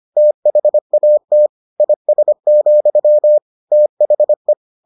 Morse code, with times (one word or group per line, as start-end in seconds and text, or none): 0.2-1.5 THAT
1.8-3.4 IS,
3.7-4.5 THE